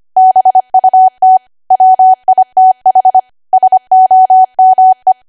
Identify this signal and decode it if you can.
Morse code, heard as a keyed tone, about 25 wpm, 750 Hz. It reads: BUT WITH SOME
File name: Archive / 25 WPM / 25WPM